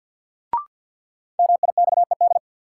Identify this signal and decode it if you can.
Morse code, heard as a keyed tone, about 47 wpm, 700 Hz. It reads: MIXED
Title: Sentences from Top 1500 Words - New Word Drill Alt - 50wpm